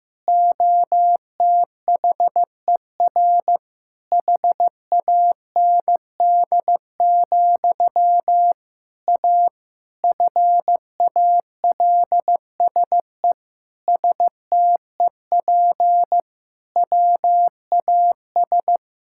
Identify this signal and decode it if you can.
Morse code, heard as a keyed tone, about 16 wpm, 700 Hz. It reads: OTHER HAND, A FALSE STEP WAS
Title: Sentences from Top 1600 Words 15wpm